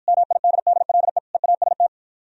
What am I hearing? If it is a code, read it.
Morse code, heard as a keyed tone, about 52 wpm, 700 Hz. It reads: MIDDLE EAST